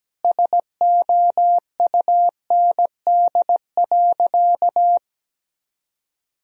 Morse code, heard as a keyed tone, about 17 wpm, 700 Hz. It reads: SOUND.
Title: Sentences from Top 1000 Words - Review Part 2 - 17wpm